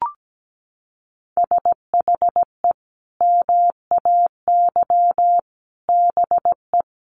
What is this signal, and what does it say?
Morse code, heard as a keyed tone, about 17 wpm, 700 Hz. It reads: SHE MAY BE
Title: Sentences from Top 1400 Words 17wpm